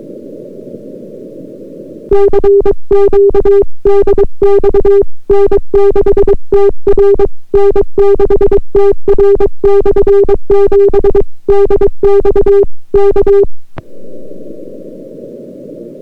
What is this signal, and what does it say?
Morse code, heard as a keyed tone, about 22 wpm, 400 Hz. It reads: CQ DX N6TR N6TR/7 DX K